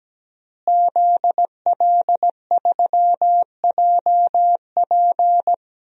Morse code, heard as a keyed tone, about 17 wpm, 700 Hz. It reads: ZL3JP
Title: DXCC-100 Call Signs - 5 Characters - Repeated 2x - 17wpm